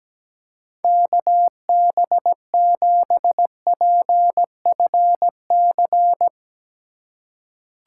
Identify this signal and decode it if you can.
Morse code, heard as a keyed tone, about 17 wpm, 700 Hz. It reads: KB7PFC
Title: USA Call Signs - 3x Character Spacing - 17wpm